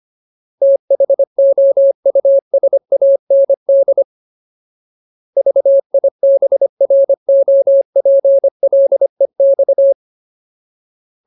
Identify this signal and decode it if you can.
Morse code, heard as a keyed tone, about 25 wpm, 550 Hz. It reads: THOUSAND VIBROPLEX
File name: Ultimate Word List / 25WPM Ultimate